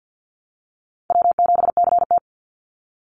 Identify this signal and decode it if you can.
Morse code, heard as a keyed tone, about 50 wpm, 700 Hz. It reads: W7CT
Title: Easy Call Signs - 50wpm